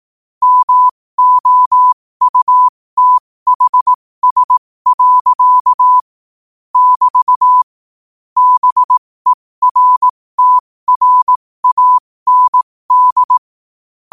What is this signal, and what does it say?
Morse code, heard as a keyed tone, about 18 wpm, spaced as for 15 wpm, 1000 Hz. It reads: MOUTHS. <BT> BERTRAND
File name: qotdfiles / 15wpm